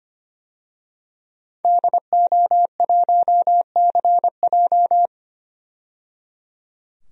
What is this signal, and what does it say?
Morse code, heard as a keyed tone, about 25 wpm, 700 Hz. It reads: DO1CJ
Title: DXCC-100 Call Signs - 5 Characters - 25wpm